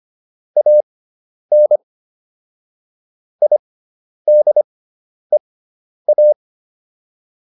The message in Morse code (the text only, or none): AN IDEA